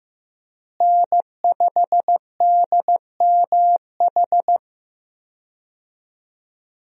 Morse code, heard as a keyed tone, about 15 wpm, 700 Hz. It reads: N5DMH
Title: USA Call Signs - 5 Characters - Repeated 2x - 15wpm